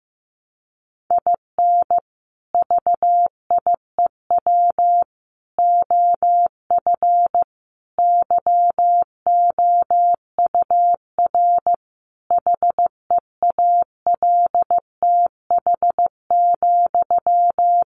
Morse code, heard as a keyed tone, about 15 wpm, 700 Hz. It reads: IN VIEW OF YOUR HEALTH,
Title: Sentences from Top 900 Words 15wpm